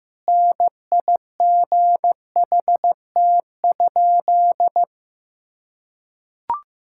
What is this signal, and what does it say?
Morse code, heard as a keyed tone, about 15 wpm, 700 Hz. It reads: NIGHT?
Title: Sentences from Top 1000 Words - Review Part 5 - 15wpm